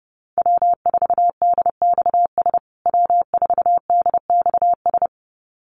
Morse code, heard as a keyed tone, about 30 wpm, 700 Hz. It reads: W4DXS W4DXS
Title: USA Call Signs Repeated 2x 30wpm